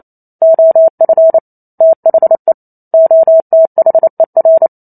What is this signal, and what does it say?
Morse code, heard as a keyed tone, about 28 wpm, 650 Hz. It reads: OF THE OTHER